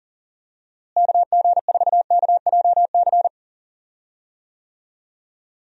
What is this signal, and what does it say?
Morse code, heard as a keyed tone, about 40 wpm, 700 Hz. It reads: KG4KJC
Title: USA Call Signs - 6 Characters - Repeated 2x - 40wpm